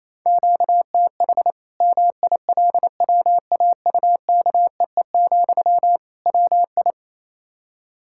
Morse code, heard as a keyed tone, about 28 wpm, 700 Hz. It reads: QTH MILWAUKEE, WI
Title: Practice QSO - Exchange 1 - Beginning - 28wpm